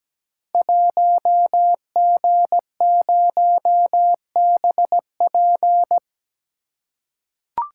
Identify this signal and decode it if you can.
Morse code, heard as a keyed tone, about 17 wpm, 700 Hz. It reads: 1G0BP
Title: Code Groups - 5 Letter:Numbers - 17wpm